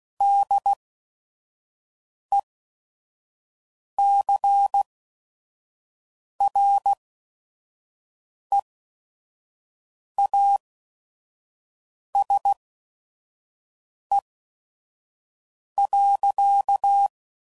Morse code, heard as a keyed tone, about 16 wpm, spaced as for 5 wpm, 800 Hz. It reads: DECREASE.
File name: CW Code Groups @5WPM / Technical Subjects / XC